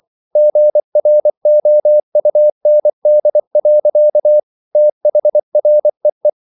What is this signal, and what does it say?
Morse code, heard as a keyed tone, about 24 wpm, 600 Hz. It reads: GROUND. THREE